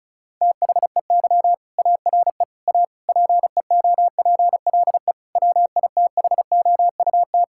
Morse code, heard as a keyed tone, about 34 wpm, 700 Hz. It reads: THEY ARE A PEOPLE WITHOUT
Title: Sentences from Top 1000 Words - Review Part 6 - 35wpm